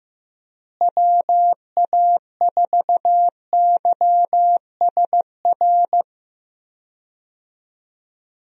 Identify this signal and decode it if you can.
Morse code, heard as a keyed tone, about 15 wpm, 700 Hz. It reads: WA4YSR
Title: USA Call Signs - 6 Characters - Repeated 3x - 15wpm